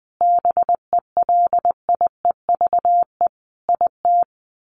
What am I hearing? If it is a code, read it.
Morse code, heard as a keyed tone, about 20 wpm, 700 Hz. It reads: BELIEVE IT
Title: Sets of 5 Words 20wpm